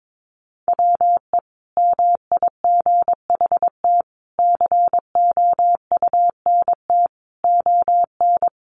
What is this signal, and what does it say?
Morse code, heard as a keyed tone, about 22 wpm, 700 Hz. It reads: WE MIGHT COUNT ON
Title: Sentences from Top 1500 Words 22wpm